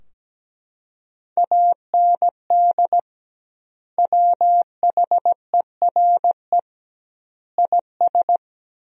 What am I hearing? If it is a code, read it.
Morse code, heard as a keyed tone, about 17 wpm, 700 Hz. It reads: AND WHERE IS
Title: 2x Word-Spacing - Sentences from Top 200 Words - 17wpm